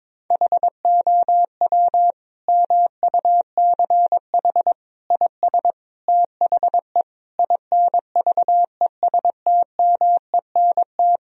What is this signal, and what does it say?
Morse code, heard as a keyed tone, about 22 wpm, 700 Hz. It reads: HOW MUCH IS THE INVESTMENT
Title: Sentences from Top 1000 Words - Review Part 4 - 22wpm